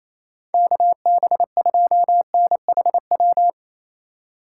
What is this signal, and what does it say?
Morse code, heard as a keyed tone, about 28 wpm, 700 Hz. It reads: KB2NHW
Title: USA Call Signs - 6 Characters - 4x Character Spacing - 28wpm